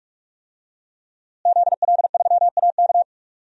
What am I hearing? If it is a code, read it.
Morse code, heard as a keyed tone, about 46 wpm, 700 Hz. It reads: ZL3AK
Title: DXCC-100 Call Signs - 5 Characters - Repeated 2x - 45wpm